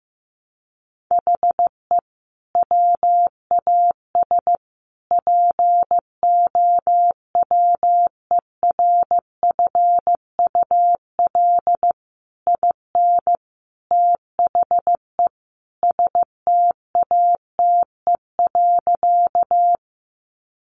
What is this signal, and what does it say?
Morse code, heard as a keyed tone, about 15 wpm, 700 Hz. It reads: HE WAS POWERFUL IN THE STATE.
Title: Sentences from Top 1300 Words 15wpm